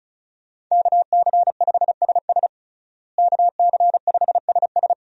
Morse code, heard as a keyed tone, about 34 wpm, 700 Hz. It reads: KC5SS KC5SS
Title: USA Call Signs - 5 Characters - Repeated 2x - 35wpm